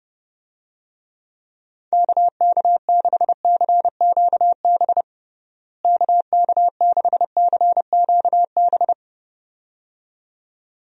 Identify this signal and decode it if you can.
Morse code, heard as a keyed tone, about 31 wpm, 700 Hz. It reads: KK6CQB KK6CQB
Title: USA Call Signs - 6 Characters - Repeated 2x - 30wpm